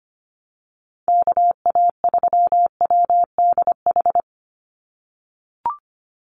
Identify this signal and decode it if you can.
Morse code, heard as a keyed tone, about 25 wpm, 700 Hz. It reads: KA3WDH
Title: USA Call Signs 25wpm